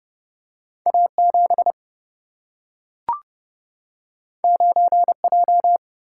Morse code, heard as a keyed tone, about 30 wpm, 700 Hz. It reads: A7 9J
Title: Single-Letter Single-Number 30wpm